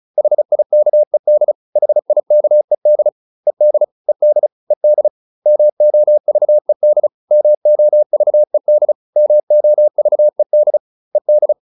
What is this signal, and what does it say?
Morse code, heard as a keyed tone, about 34 wpm, 600 Hz. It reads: HIKED HIKED ED ED ED MOVED MOVED MOVED ED